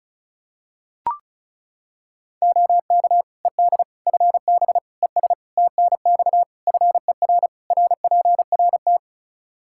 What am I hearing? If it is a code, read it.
Morse code, heard as a keyed tone, about 36 wpm, 700 Hz. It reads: OK ED FB ES TNX FER RPRT
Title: K1USN SST QSO Practice 35wpm